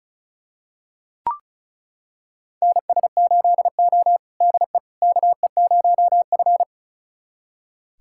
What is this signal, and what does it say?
Morse code, heard as a keyed tone, about 34 wpm, 700 Hz. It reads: NS8O DE KE0F
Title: USA Callsign DE Callsign - 4 Characters - 35wpm